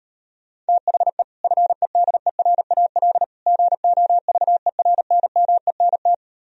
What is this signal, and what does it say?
Morse code, heard as a keyed tone, about 38 wpm, 700 Hz. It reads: THE FEDERAL GOVERNMENT